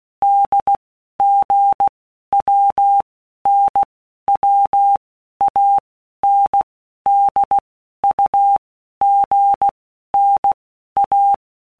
Morse code, heard as a keyed tone, about 16 wpm, 800 Hz. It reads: D G W N W A N D U G N A
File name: All Groups 1-minute Blocks / Groups@15 / AWUNDG